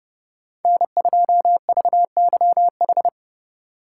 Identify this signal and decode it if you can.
Morse code, heard as a keyed tone, about 28 wpm, 700 Hz. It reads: N2VYH